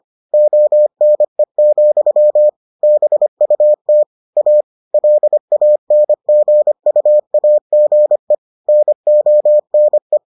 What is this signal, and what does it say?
Morse code, heard as a keyed tone, about 25 wpm, 600 Hz. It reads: ONE, BUT A LANGUAGE NONE